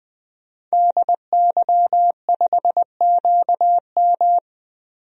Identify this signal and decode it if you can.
Morse code, heard as a keyed tone, about 21 wpm, 700 Hz. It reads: DY5QM
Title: DXCC-100 Call Signs - 5 Characters - 20wpm